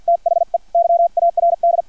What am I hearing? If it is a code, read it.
Morse code, heard as a keyed tone, about 50 wpm, spaced as for 44 wpm, 680 Hz, with radio noise. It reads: THE YARD
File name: archived / CRSnet